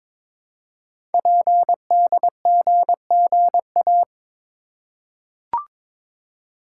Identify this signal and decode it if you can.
Morse code, heard as a keyed tone, about 22 wpm, 700 Hz. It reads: PDGGA
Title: Code Groups - 5 Letters - 22wpm